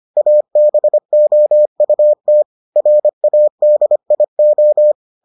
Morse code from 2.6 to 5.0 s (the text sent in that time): RADIO